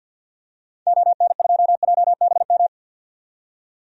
Morse code, heard as a keyed tone, about 52 wpm, 700 Hz. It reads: ON2JBM
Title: DXCC-100 Call Signs - 6 Characters - 50wpm